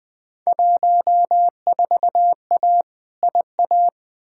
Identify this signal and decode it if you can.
Morse code, heard as a keyed tone, about 20 wpm, 700 Hz. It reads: 14A IA
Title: Field Day Exchanges Encore - 20wpm